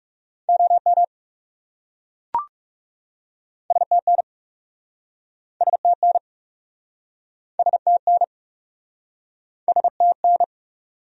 Morse code, heard as a keyed tone, about 40 wpm, 700 Hz. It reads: OM STN STN STN STN